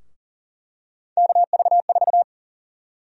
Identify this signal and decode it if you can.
Morse code, heard as a keyed tone, about 40 wpm, 700 Hz. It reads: KV4
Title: USA Call Sign Prefixes - 40wpm